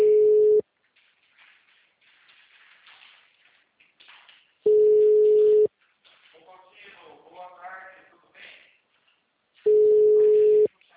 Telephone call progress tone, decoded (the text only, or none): ringback tone